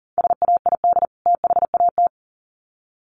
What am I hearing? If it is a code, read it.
Morse code, heard as a keyed tone, about 40 wpm, 700 Hz. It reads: SAID THAT
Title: Sets of 2 Words 40wpm